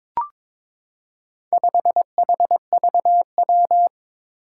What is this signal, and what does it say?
Morse code, heard as a keyed tone, about 22 wpm, 700 Hz. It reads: 5HVW
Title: Four Challenging Letter-Number Combinations 22wpm